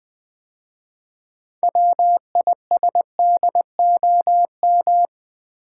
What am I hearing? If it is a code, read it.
Morse code, heard as a keyed tone, about 20 wpm, 700 Hz. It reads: WISDOM
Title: 6 Letter Words - First 3 Letters Given - 20wpm